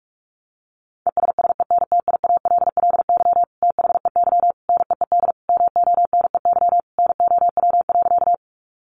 Morse code, heard as a keyed tone, about 44 wpm, 700 Hz. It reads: ESSENTIALLY THEY NEED MONEY NOW.
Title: Sentences from Top 1300 Words 45wpm